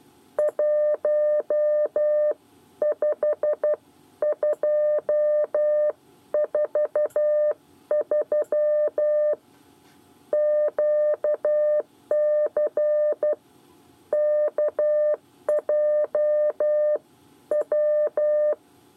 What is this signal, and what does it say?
Morse code, heard as a keyed tone, about 12 wpm, spaced as for 10 wpm, 600 Hz. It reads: 15243 QCKJW
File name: 8 bis lezione cw 1